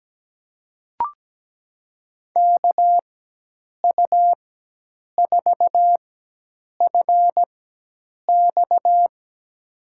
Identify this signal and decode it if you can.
Morse code, heard as a keyed tone, about 17 wpm, spaced as for 8 wpm, 700 Hz. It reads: KU4FX
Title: USA Call Signs - 5 Characters - 4x Character Spacing - 17wpm